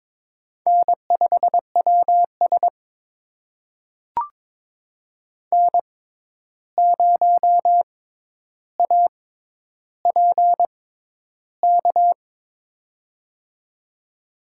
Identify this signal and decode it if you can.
Morse code, heard as a keyed tone, about 22 wpm, spaced as for 11 wpm, 700 Hz. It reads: N5WS N0APK
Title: USA Call Signs - 6x Character Spacing - 22wpm